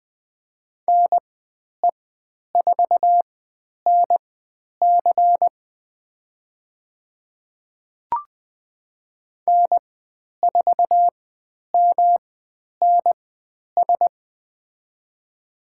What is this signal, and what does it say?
Morse code, heard as a keyed tone, about 20 wpm, spaced as for 10 wpm, 700 Hz. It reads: NE4NC N4MNS
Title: 109 Lesson 18g - Letter m - Call Signs 20wpm